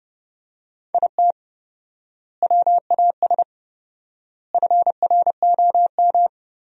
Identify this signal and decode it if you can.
Morse code, heard as a keyed tone, about 30 wpm, 700 Hz. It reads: IT WAS FROM
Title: 4x Word-Spacing - Sentences from Top 100 Words - 30wpm